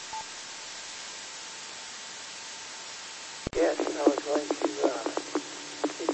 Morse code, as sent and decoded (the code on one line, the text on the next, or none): none